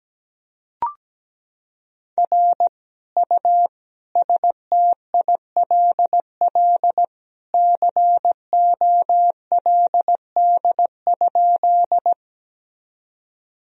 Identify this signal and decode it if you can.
Morse code, heard as a keyed tone, about 17 wpm, 700 Hz. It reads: R U STILL COLD?
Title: Sentences from Top 400 Words and Abbreviations 17wpm